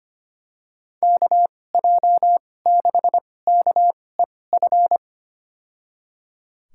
Morse code, heard as a keyed tone, about 25 wpm, spaced as for 18 wpm, 700 Hz. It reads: KJ6KEF
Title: USA Call Signs - 6 Characters - 2x Character Spacing - 25wpm